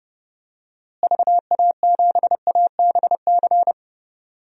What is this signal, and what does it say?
Morse code, heard as a keyed tone, about 30 wpm, 700 Hz. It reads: VA7ABC
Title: DXCC-100 Call Signs - 6 Characters - Repeated 2x - 30wpm